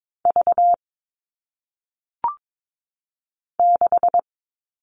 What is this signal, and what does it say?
Morse code, heard as a keyed tone, about 22 wpm, 700 Hz. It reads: V 6
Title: All Cut Numbers - 1-Digit 22wpm